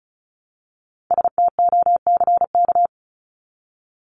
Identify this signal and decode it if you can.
Morse code, heard as a keyed tone, about 34 wpm, 700 Hz. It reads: STOCK
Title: Top 501-750 Words 35wpm